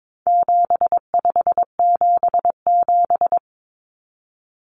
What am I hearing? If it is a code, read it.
Morse code, heard as a keyed tone, about 22 wpm, 700 Hz. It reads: 7577
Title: Common Cut Numbers - 4-Digit 22wpm